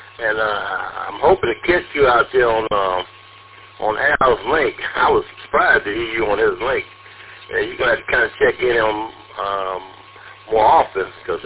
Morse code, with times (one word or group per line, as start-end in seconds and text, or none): none